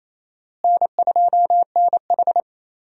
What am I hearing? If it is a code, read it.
Morse code, heard as a keyed tone, about 28 wpm, 700 Hz. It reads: N2NH
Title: USA Call Signs - 4 Characters - Repeated 2x - 28wpm